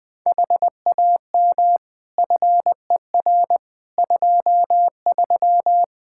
Morse code, heard as a keyed tone, about 20 wpm, 700 Hz. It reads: HAM FER 23
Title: Practice QSO - Exchange 3 - More Info - 20wpm